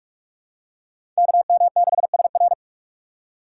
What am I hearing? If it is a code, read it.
Morse code, heard as a keyed tone, about 46 wpm, 700 Hz. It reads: KM6SR
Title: USA Call Signs - 5 Characters - 3x Character Spacing - 45wpm